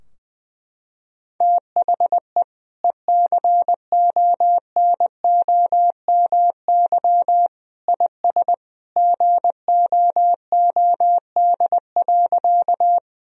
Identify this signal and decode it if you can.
Morse code, heard as a keyed tone, about 20 wpm, 700 Hz. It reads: THE ECONOMY IS GOOD.
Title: Sentences from Top 1000 Words - Review Part 3 - 20wpm